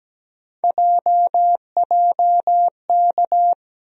Morse code, heard as a keyed tone, about 17 wpm, 700 Hz. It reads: JJK
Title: USA Call Sign Postfixes - 3 Letters 17wpm